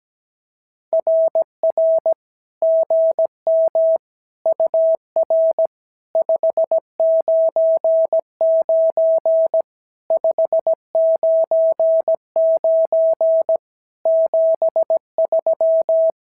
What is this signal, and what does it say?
Morse code, heard as a keyed tone, about 17 wpm, 690 Hz. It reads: RR GM UR 599 599 73
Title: SOTA QSO Practice 17wpm